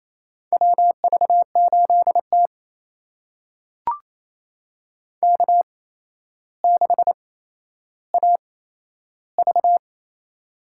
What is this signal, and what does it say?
Morse code, heard as a keyed tone, about 28 wpm, spaced as for 12 wpm, 700 Hz. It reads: WV8T K6AV
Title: USA Call Signs - 4 Characters - 8x Character Spacing - 28wpm